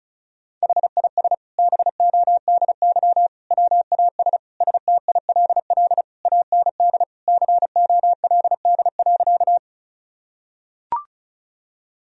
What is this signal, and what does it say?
Morse code, heard as a keyed tone, about 34 wpm, 700 Hz. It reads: HIS BODY WAS STILL AND COLD.
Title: Sentences from Top 1000 Words - Review Part 2 - 35wpm